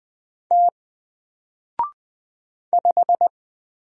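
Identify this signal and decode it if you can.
Morse code, heard as a keyed tone, about 20 wpm, 700 Hz. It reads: T 5
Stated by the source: Single letter and number Morse drill at 20 wpm, from N4LCD.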